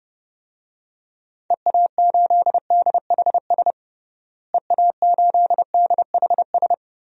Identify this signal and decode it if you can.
Morse code, heard as a keyed tone, about 30 wpm, 700 Hz. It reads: EA8DHS EA8DHS
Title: DXCC-100 Call Signs - 6 Characters - Repeated 2x - 30wpm